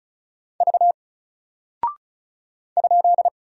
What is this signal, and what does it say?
Morse code, heard as a keyed tone, about 36 wpm, 700 Hz. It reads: V ?
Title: Single Character (Letter:Number:Punctuation:Prosign) 35wpm